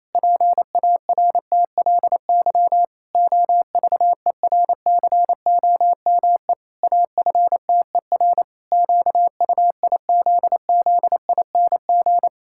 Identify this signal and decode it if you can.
Morse code, heard as a keyed tone, about 28 wpm, 700 Hz. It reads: PARTLY OVERCOME AFTER QUIZZING
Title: Pangram Challenge v2 - 28wpm